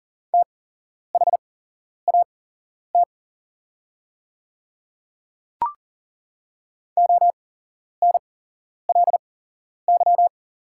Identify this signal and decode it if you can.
Morse code, heard as a keyed tone, about 40 wpm, spaced as for 11 wpm, 700 Hz. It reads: THAT ONLY
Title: 8x Character Spacing - Top 500 Words - 40wpm